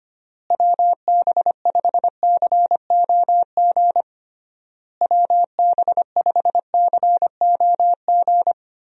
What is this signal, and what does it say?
Morse code, heard as a keyed tone, about 25 wpm, 700 Hz. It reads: WB5COG WB5COG
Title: USA Call Signs - 6 Characters - Repeated 3x - 25wpm